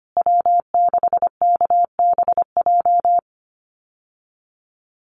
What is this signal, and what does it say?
Morse code, heard as a keyed tone, about 25 wpm, 700 Hz. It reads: W6KBJ
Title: USA Call Signs Repeated 3x 25wpm